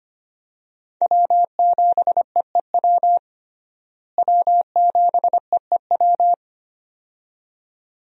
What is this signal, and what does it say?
Morse code, heard as a keyed tone, about 25 wpm, 700 Hz. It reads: W7EEW W7EEW
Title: USA Call Signs - 5 Characters - Repeated 2x - 25wpm